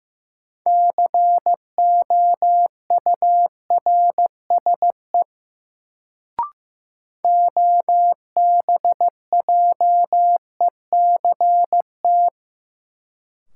Morse code, heard as a keyed tone, about 15 wpm, 700 Hz. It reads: COURSE OBJECT
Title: Sentences from Top 500 Words - New Word Drill - 15wpm